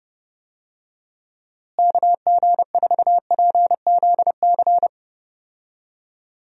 Morse code, heard as a keyed tone, about 30 wpm, 700 Hz. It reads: KG4PZC
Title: USA Call Signs - 6 Characters - Repeated 3x - 30wpm